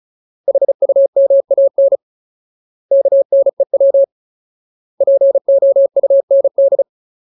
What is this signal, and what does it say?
Morse code, heard as a keyed tone, about 37 wpm, 550 Hz. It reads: HUMAN KNEW POUND
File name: Ultimate Word List / 35WPM Ultimate